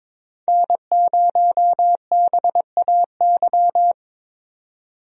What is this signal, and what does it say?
Morse code, heard as a keyed tone, about 22 wpm, 700 Hz. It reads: N0BAY
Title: USA Call Signs - 5 Characters - 22wpm